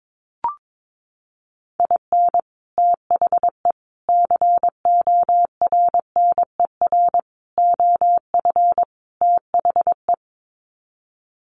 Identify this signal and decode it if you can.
Morse code, heard as a keyed tone, about 22 wpm, 700 Hz. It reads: IN THE CORNER OF THE
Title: Sets of 5 Words Encore 22wpm